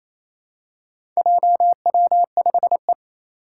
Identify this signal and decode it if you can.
Morse code, heard as a keyed tone, about 27 wpm, 700 Hz. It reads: JW5E
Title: Four Challenging Letter-Number Combinations 28wpm